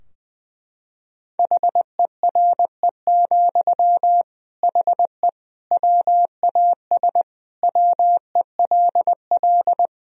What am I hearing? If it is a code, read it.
Morse code, heard as a keyed tone, about 21 wpm, 700 Hz. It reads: HERE, HE WAS WELL
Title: Sentences from Top 1000 Words - Review Part 3 - 20wpm